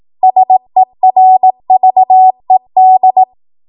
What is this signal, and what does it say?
Morse code, heard as a keyed tone, about 18 wpm, 750 Hz. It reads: SERVED